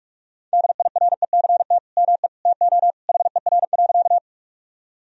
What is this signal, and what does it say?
Morse code, heard as a keyed tone, about 41 wpm, 700 Hz. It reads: DIRECT ME TO HER.
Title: Sentences from Top 1000 Words - Review Part 2 - 45wpm